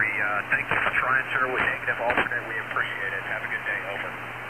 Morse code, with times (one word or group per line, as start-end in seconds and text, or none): none